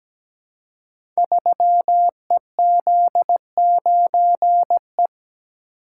Morse code, heard as a keyed tone, about 17 wpm, 700 Hz. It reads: 3EZ9E
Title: Code Groups - 5 Letter:Numbers - 17wpm